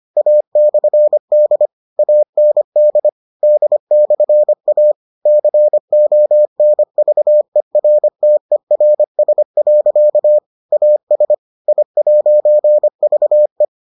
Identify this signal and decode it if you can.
Morse code, heard as a keyed tone, about 25 wpm, 600 Hz. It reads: A/D AND D/A CONVERTERS. AS I'VE